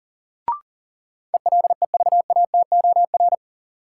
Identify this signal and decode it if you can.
Morse code, heard as a keyed tone, about 40 wpm, 700 Hz. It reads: ELEVATOR
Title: Sentences from Top 1600 Words - New Word Drill - 40wpm